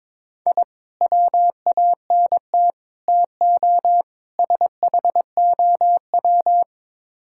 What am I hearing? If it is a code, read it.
Morse code, heard as a keyed tone, about 22 wpm, 700 Hz. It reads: I WANT TO SHOW
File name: practice / 22wpm / Build A Phrase - 5-Word Phrases - 22wpm